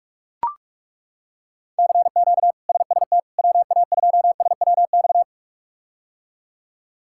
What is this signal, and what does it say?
Morse code, heard as a keyed tone, about 41 wpm, 700 Hz. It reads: CQ SST WA2SWX
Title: K1USN SST QSO Practice - 45wpm